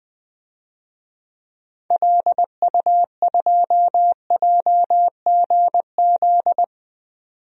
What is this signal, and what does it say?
Morse code, heard as a keyed tone, about 20 wpm, 700 Hz. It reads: LU2JGZ
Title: DXCC-100 Call Signs - 6 Characters - 20wpm